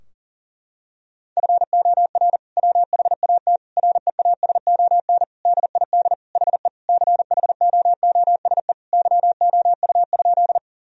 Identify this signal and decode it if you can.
Morse code, heard as a keyed tone, about 40 wpm, 700 Hz. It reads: FOR WHAT REASON DID HE CHOOSE YOU?
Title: Sentences from Top 1000 Words - Review Part 4 - 40wpm